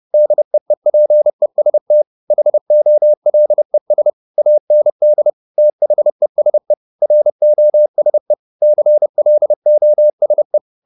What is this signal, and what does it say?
Morse code, heard as a keyed tone, about 30 wpm, 600 Hz. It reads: DEEPEST HOLES AND THESE ROSE CLOSE
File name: the old man and the sea / the old man and the sea 1 30wpm